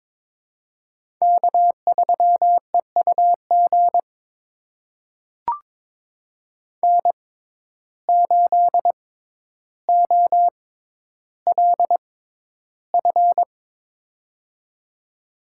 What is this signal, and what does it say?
Morse code, heard as a keyed tone, about 22 wpm, spaced as for 11 wpm, 700 Hz. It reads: K3EUG N8OLF
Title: USA Call Signs - 5 Characters - 6x Character Spacing - 22wpm